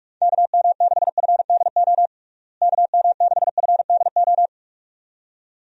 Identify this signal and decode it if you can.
Morse code, heard as a keyed tone, about 43 wpm, 700 Hz. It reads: KM6FDO KM6FDO
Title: USA Call Signs - 6 Characters - Repeated 2x - 45wpm